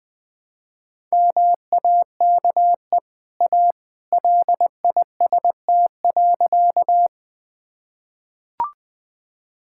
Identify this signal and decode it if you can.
Morse code, heard as a keyed tone, about 21 wpm, 700 Hz. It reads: MAKE A LIST.
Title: Easy Sentences with Hints - 20wpm